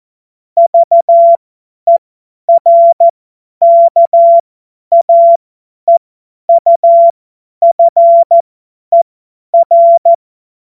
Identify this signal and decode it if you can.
Morse code, heard as a keyed tone, about 14 wpm, spaced as for 10 wpm, 680 Hz. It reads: VERKAEUFER
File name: Morsekurs / Witze / Witz 08. August 2023 50 F70